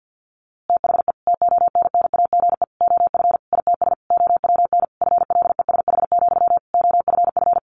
Morse code, heard as a keyed tone, about 48 wpm, 700 Hz. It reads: THE TONNAGE OF ITS OWN FLESH, OFF
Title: A Sound Of Thunder 50wpm